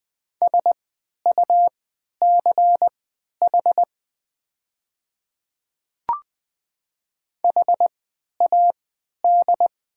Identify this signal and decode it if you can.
Morse code, heard as a keyed tone, about 20 wpm, spaced as for 11 wpm, 690 Hz. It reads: SUCH HAD
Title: 3x Character Spacing - Top 200 Words - 20wpm